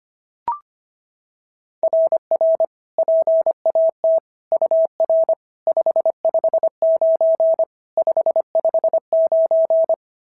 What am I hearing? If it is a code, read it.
Morse code, heard as a keyed tone, about 22 wpm, 650 Hz. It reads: RR PAT UR 559 559
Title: SOTA QSO Practice 25wpm